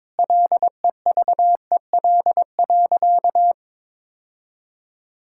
Morse code, heard as a keyed tone, about 22 wpm, 700 Hz. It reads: LEVEL.
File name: Sentences with 7-Word Phrases - 22wpm